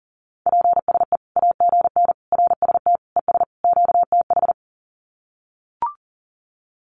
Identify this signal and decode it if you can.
Morse code, heard as a keyed tone, about 40 wpm, 700 Hz. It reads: PSE AGN RST ES QTH
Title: Common QSO Phrases 40wpm